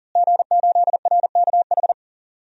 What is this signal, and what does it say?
Morse code, heard as a keyed tone, about 40 wpm, 690 Hz. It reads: G8RKH
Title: DXCC-100 Call Signs - 5 Characters - Repeated 2x - 40wpm